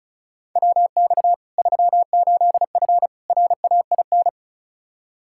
Morse code, heard as a keyed tone, about 36 wpm, 700 Hz. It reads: WX 38F RAIN
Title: Practice QSO - Exchange 2 - Info - 35wpm